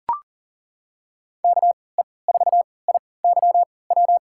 Morse code, heard as a keyed tone, about 40 wpm, spaced as for 23 wpm, 700 Hz. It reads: KE4IYW
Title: USA Call Signs - 6 Characters - 3x Character Spacing - 40wpm